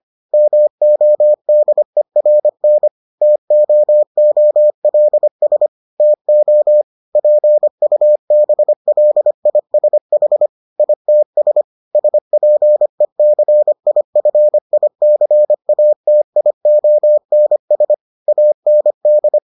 Morse code, heard as a keyed tone, about 25 wpm, 600 Hz. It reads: MODERN TOOLS TO PUBLISH ITS SPECIFICATIONS AND